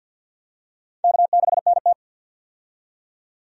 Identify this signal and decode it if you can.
Morse code, heard as a keyed tone, about 50 wpm, 700 Hz. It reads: K6NT
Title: USA Call Signs - 4 Characters - 50wpm